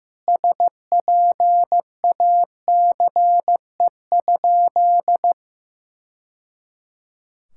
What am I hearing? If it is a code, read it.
Morse code, heard as a keyed tone, about 15 wpm, 700 Hz. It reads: SPACE?
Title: Sentences from Top 1000 Words - Review Part 5 - 15wpm